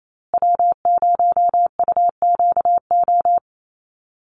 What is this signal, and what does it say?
Morse code, heard as a keyed tone, about 27 wpm, 700 Hz. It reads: W0UQO
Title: USA Call Signs Repeated 2x 28wpm